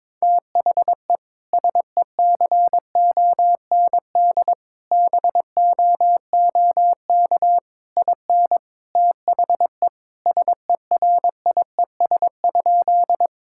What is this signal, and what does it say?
Morse code, heard as a keyed tone, about 22 wpm, 700 Hz. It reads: THE SECOND BOOK IN THE SERIES?
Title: Sentences from Top 1000 Words - Review Part 5 - 22wpm